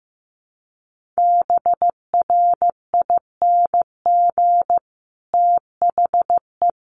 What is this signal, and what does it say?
Morse code, heard as a keyed tone, about 15 wpm, 700 Hz. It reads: BRING THE
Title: Sentences from Top 1500 Words 15wpm